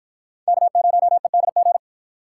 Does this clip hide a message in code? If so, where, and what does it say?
Yes: Morse code from 0.5 to 1.8 s, heard as a keyed tone, about 53 wpm, 700 Hz. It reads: K0EDG